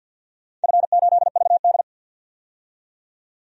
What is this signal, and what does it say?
Morse code, heard as a keyed tone, about 47 wpm, 700 Hz. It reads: F8VD
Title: DXCC-100 Call Signs - 4 Characters - 50wpm